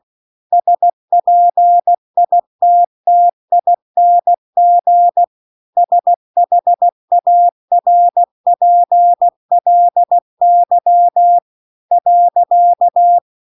Morse code, heard as a keyed tone, about 16 wpm, 700 Hz. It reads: SPITTING SHARPLY .